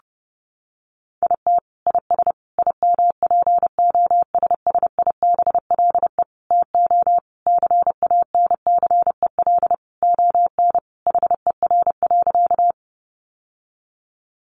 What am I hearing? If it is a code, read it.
Morse code, heard as a keyed tone, about 30 wpm, 700 Hz. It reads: IT IS IMPOSSIBLE TO CANCEL ON HER.
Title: Sentences from Top 1200 Words 30wpm